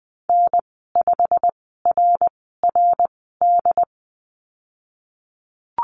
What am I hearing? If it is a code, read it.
Morse code, heard as a keyed tone, about 19 wpm, spaced as for 14 wpm, 700 Hz. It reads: N5RRD
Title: 128 Lesson 21g - Number 6 - Call Signs 20wpm